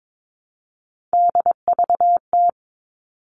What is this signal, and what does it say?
Morse code, heard as a keyed tone, about 22 wpm, 700 Hz. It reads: DVT
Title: All Cut Numbers - 3-Digit 22wpm